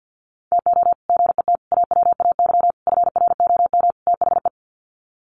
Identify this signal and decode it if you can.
Morse code, heard as a keyed tone, about 50 wpm, 700 Hz. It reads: TO GET AWAY FROM THE
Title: Sets of 5 Words Encore2 50wpm